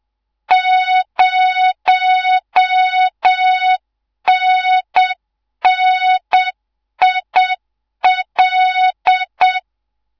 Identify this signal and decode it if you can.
Morse code, heard as a keyed tone, about 7 wpm, 3700 Hz, with noise. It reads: 0NNIL
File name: Morze-HA2MN / hangp / 2 03 GYAKORLAT